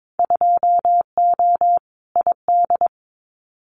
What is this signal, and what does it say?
Morse code, heard as a keyed tone, about 22 wpm, 700 Hz. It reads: 2O ID
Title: Winter Field Day Exchanges 22wpm